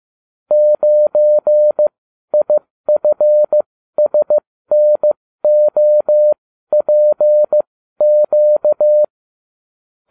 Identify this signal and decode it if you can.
Morse code, heard as a keyed tone, about 15 wpm, spaced as for 12 wpm, 600 Hz. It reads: 9IFSNOPQ